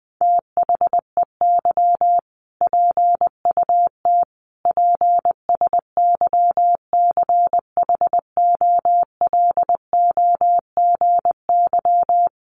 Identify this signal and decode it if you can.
Morse code, heard as a keyed tone, about 20 wpm, 700 Hz. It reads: THEY PUT PSYCHOLOGY
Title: Sentences from Top 1400 Words 20wpm